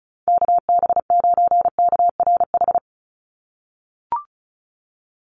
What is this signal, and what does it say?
Morse code, heard as a keyed tone, about 34 wpm, 700 Hz. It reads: KB9KRH
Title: USA Call Signs 35wpm